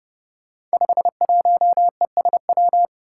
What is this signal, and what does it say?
Morse code, heard as a keyed tone, about 30 wpm, 700 Hz. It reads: 51ESW
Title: Five Challenging Letter-Number Combinations 30wpm